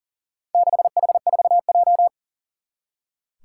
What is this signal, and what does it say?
Morse code, heard as a keyed tone, about 40 wpm, 700 Hz. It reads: 6H4J
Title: Four Challenging Letter-Number Combinations 40wpm